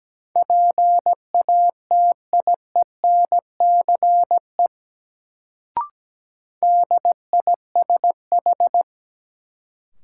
Morse code, heard as a keyed tone, about 17 wpm, 700 Hz. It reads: PATIENCE DISH
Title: Sentences from Top 1300 Words - New Word Drill - 17wpm